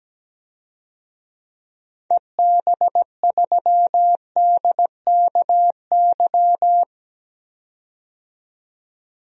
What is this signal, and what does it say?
Morse code, heard as a keyed tone, about 17 wpm, 700 Hz. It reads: EB3DKY
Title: DXCC-100 Call Signs - 6 Characters - 17wpm